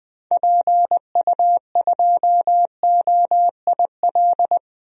Morse code, heard as a keyed tone, about 20 wpm, 700 Hz. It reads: PU2OIL
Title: DXCC-100 Call Signs - 6 Characters - 20wpm